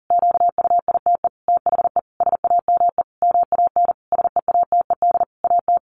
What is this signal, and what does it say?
Morse code, heard as a keyed tone, about 40 wpm, 700 Hz. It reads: QUITE THE SAME MAN SEATED AT
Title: A Sound Of Thunder 40wpm